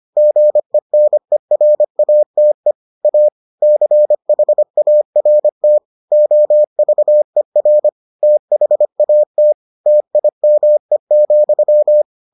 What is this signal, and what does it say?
Morse code, heard as a keyed tone, about 25 wpm, 600 Hz. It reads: GENERATE A CHART OVER THAT TIME,